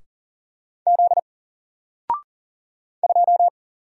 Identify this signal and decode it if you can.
Morse code, heard as a keyed tone, about 40 wpm, 700 Hz. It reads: Z 2
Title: Single Letter-Number Alt 40wpm